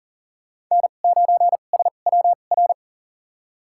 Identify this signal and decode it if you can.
Morse code, heard as a keyed tone, about 40 wpm, spaced as for 29 wpm, 700 Hz. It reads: N9SWR
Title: USA Call Signs - 5 Characters - 2x Character Spacing - 40wpm